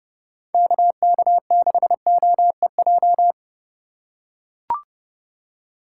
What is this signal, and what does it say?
Morse code, heard as a keyed tone, about 30 wpm, 700 Hz. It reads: KK6OEJ
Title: USA Call Signs - 6 Characters - 4x Character Spacing - 30wpm